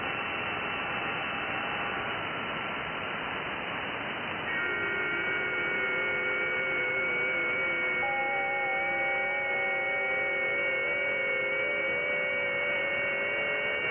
FT8.
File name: IC-7100